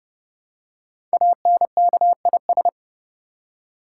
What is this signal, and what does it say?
Morse code, heard as a keyed tone, about 30 wpm, 700 Hz. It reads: ANKIS
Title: Code Groups - 5 Letters - 30wpm